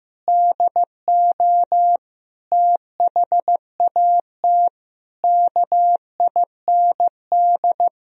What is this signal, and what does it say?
Morse code, heard as a keyed tone, about 15 wpm, 700 Hz. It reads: DO THAT KIND
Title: Sentences from Top 1000 Words - Review Part 2 - 15wpm